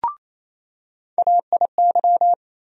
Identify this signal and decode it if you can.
Morse code, heard as a keyed tone, about 28 wpm, 700 Hz. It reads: AIY
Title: USA Call Sign Postfixes - 3 Letters 28wpm